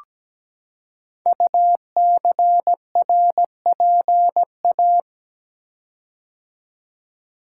Morse code, heard as a keyed tone, about 17 wpm, 700 Hz. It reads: UCRPA
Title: Code Groups - 5 Letter:Numbers - 17wpm